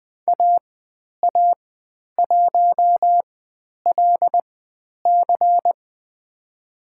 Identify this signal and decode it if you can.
Morse code, heard as a keyed tone, about 20 wpm, spaced as for 10 wpm, 700 Hz. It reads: AA1LC
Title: 095 Lesson 16g - Letter c - Call Signs 20wpm